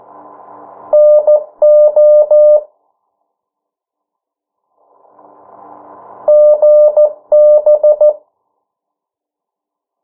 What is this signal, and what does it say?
Morse code, heard as a keyed tone, about 14 wpm, 600 Hz. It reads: NO GB
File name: morse mp3 / 09 urok 8